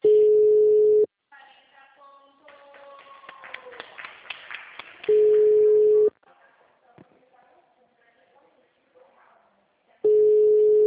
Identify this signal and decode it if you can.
Ringback tone (Europe, 425 Hz), with speech.
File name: out- -2004- - - .36278